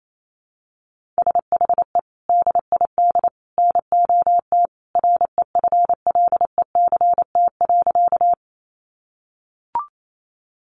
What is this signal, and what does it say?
Morse code, heard as a keyed tone, about 28 wpm, 700 Hz. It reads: SHE DID NOT REFLECT.
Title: Sentences from Top 1500 Words 28wpm